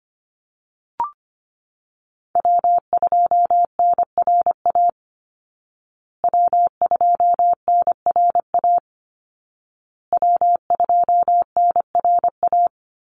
Morse code, heard as a keyed tone, about 25 wpm, 700 Hz. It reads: W2NRA W2NRA W2NRA
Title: Call Signs Repeated 2x - 25wpm